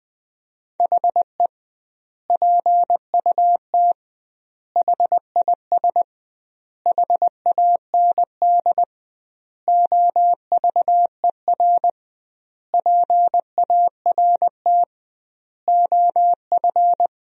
Morse code, heard as a keyed tone, about 20 wpm, 700 Hz. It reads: HE PUT HIS HAND OVER PART OF
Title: 2x Word-Spacing - Sentences from Top 400 Words - 20wpm